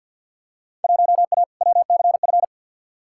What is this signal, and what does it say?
Morse code, heard as a keyed tone, about 48 wpm, 700 Hz. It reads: 1A WCF
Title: Field Day Exchanges - 50wpm